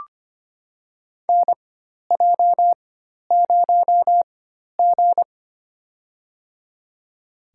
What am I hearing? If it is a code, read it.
Morse code, heard as a keyed tone, about 25 wpm, spaced as for 12 wpm, 700 Hz. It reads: NJ0G
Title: USA Call Signs - 4 Characters - 4x Character Spacing - 25wpm